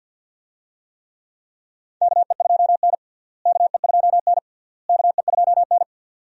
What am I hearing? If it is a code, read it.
Morse code, heard as a keyed tone, about 51 wpm, 700 Hz. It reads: KE2N KE2N KE2N
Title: USA Call Signs - 4 Characters - Repeated 3x - 50wpm